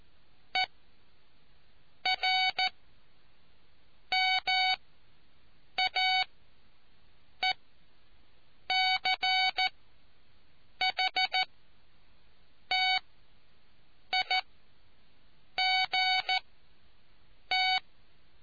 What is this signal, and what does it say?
Morse code, heard as a keyed tone, about 14 wpm, spaced as for 6 wpm, 750 Hz. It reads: ERMAECHTIGT